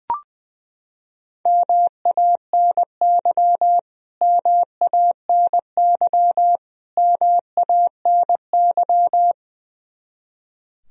Morse code, heard as a keyed tone, about 21 wpm, 700 Hz. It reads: MANY MANY MANY
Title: Top 100 Words Repeated 20wpm